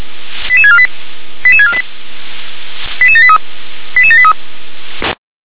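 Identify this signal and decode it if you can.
Five-tone selective call (ZVEI), heard as quick runs of five tones, tones of about 70 ms. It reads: 88648 88648 88783 88783
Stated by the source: ZVEI five-tone call received by SDR.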